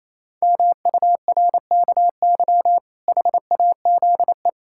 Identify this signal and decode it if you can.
Morse code, heard as a keyed tone, about 28 wpm, 700 Hz. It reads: MURKY HAZE
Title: Pangram Challenge v2 - 28wpm